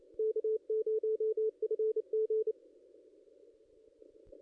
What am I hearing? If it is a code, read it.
Morse code, heard as a keyed tone, about 31 wpm, 440 Hz, with radio noise. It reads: K0FG
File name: contests / ss-cw-k5tr-op-n5rz -0400